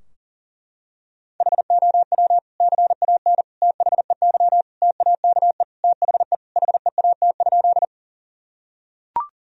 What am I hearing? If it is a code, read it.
Morse code, heard as a keyed tone, about 40 wpm, 700 Hz. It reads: HOW CAN THEY TAKE THE HEAT?
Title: Sentences from Top 1000 Words - Review Part 3 - 40wpm